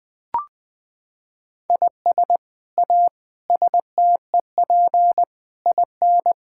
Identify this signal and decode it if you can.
Morse code, heard as a keyed tone, about 20 wpm, 700 Hz. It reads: IS A STEP IN
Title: Sets of 8 Words - 20wpm